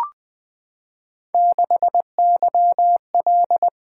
Morse code, heard as a keyed tone, about 21 wpm, 700 Hz. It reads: 6YL
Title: Three Letter-Number Combinations - 20wpm